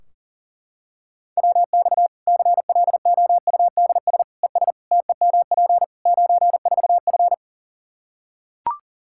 Morse code, heard as a keyed tone, about 40 wpm, 700 Hz. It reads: WX CLOUDS ES TEMP 94F
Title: Practice QSO - Exchange 2 - Info - 40wpm